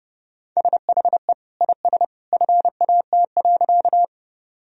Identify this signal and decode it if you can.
Morse code, heard as a keyed tone, about 30 wpm, 700 Hz. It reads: SHE IS FAT.
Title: Sentences from Top 1000 Words - Review Part 4 - 30wpm